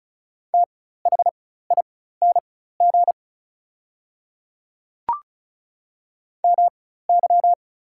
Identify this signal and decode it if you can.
Morse code, heard as a keyed tone, about 36 wpm, spaced as for 16 wpm, 700 Hz. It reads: THING MY
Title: 4x Character Spacing - Top 100 Words - 35wpm